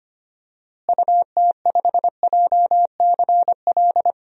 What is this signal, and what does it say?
Morse code, heard as a keyed tone, about 25 wpm, 700 Hz. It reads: UT5JCL
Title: DXCC-100 Call Signs - 6 Characters - Repeated 2x - 25wpm